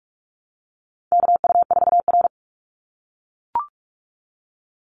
Morse code, heard as a keyed tone, about 44 wpm, 700 Hz. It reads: KU4R
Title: NA SOTA Chaser Callsigns 45wpm